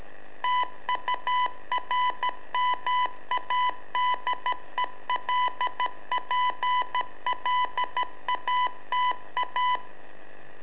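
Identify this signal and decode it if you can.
Morse code, heard as a keyed tone, about 13 wpm, 1000 Hz. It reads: TURMADELPLATA
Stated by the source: Telemetry in Morse from LU7AA and LU4EZ, AMSAT and Turma Mar del Plata, sent from a balloon.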